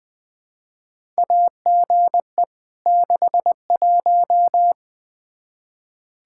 Morse code, heard as a keyed tone, about 20 wpm, 700 Hz. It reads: AGE 61
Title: Practice QSO - Exchange 3 - More Info - 20wpm